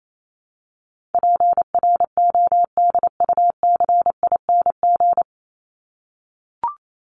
Morse code, heard as a keyed tone, about 28 wpm, 700 Hz. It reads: PRODUCING
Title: ING Suffix - Top 100 Words with Base Words 28wpm